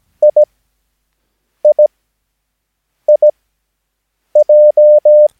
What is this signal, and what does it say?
Morse code, heard as a keyed tone, about 17 wpm, spaced as for 6 wpm, 600 Hz. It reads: IIIJ
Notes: Övningsmaterial med alfabetet A till Ö samt siffror 0 till 9 enligt International/Continental Morse Code.